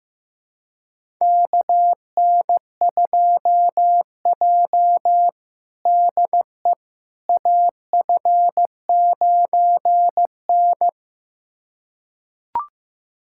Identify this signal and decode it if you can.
Morse code, heard as a keyed tone, about 16 wpm, 700 Hz. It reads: KN2J DE AF9N
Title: USA Callsign DE Callsign - 4 Characters - 15wpm